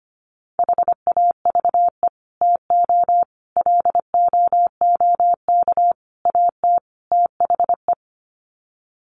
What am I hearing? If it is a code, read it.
Morse code, heard as a keyed tone, about 25 wpm, 700 Hz. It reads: HAVE TO LOOK AT THE
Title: Sets of 5 Words Encore 25wpm